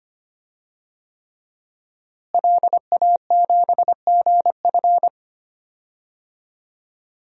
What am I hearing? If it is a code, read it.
Morse code, heard as a keyed tone, about 25 wpm, 700 Hz. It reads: LA7GF
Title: DXCC-100 Call Signs - 5 Characters - Repeated 3x - 25wpm